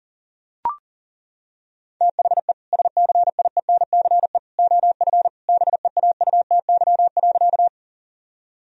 Morse code, heard as a keyed tone, about 40 wpm, 700 Hz. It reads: THE SCIENCE OF BEAUTY.
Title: Sentences from Top 1000 Words - Review Part 2 - 40wpm